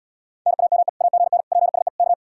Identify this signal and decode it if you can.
Morse code, heard as a keyed tone, about 79 wpm, 700 Hz. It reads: NAME AGN PSE K